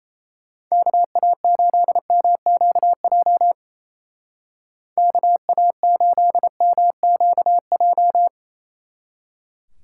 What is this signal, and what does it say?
Morse code, heard as a keyed tone, about 36 wpm, 700 Hz. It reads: KA8MQJ KA8MQJ